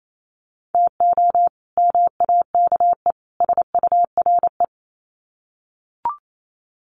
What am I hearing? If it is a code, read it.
Morse code, heard as a keyed tone, about 28 wpm, 700 Hz. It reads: TO MAKE SURE
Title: Sets of 3 Words 28wpm